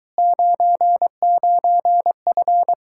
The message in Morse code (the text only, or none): 99F